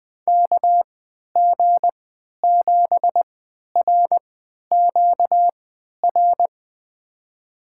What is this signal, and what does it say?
Morse code, heard as a keyed tone, about 20 wpm, spaced as for 11 wpm, 700 Hz. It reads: KG7RQR
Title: USA Call Signs - 6 Characters - 3x Character Spacing - 20wpm